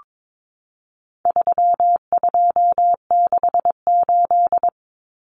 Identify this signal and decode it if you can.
Morse code, heard as a keyed tone, about 22 wpm, 700 Hz. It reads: 3268
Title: Numbers - 4-Digit 22wpm